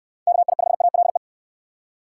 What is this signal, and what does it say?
Morse code, heard as a keyed tone, about 69 wpm, 700 Hz. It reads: KI5ALE